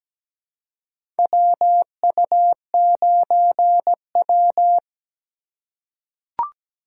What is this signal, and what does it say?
Morse code, heard as a keyed tone, about 17 wpm, 700 Hz. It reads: WU9W
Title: USA Call Signs - 4 Characters - 2x Character Spacing - 17wpm